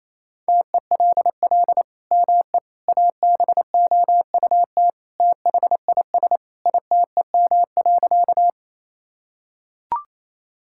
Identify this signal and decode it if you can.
Morse code, heard as a keyed tone, about 28 wpm, 700 Hz. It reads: TELL ME ABOUT THIS ITEM.
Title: Sentences from Top 1000 Words - Review Part 6 - 28wpm